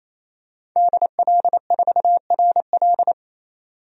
Morse code, heard as a keyed tone, about 28 wpm, 700 Hz. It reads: DL4RL
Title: DXCC-100 Call Signs - 5 Characters - 28wpm